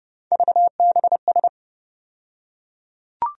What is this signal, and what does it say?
Morse code, heard as a keyed tone, about 30 wpm, 700 Hz. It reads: VBS
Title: USA Call Sign Postfixes - 3 Letters 30wpm